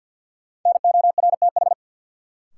Morse code, heard as a keyed tone, about 49 wpm, 700 Hz. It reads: NORTH
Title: Sets of 5 Words Encore - New Word Drill - 50wpm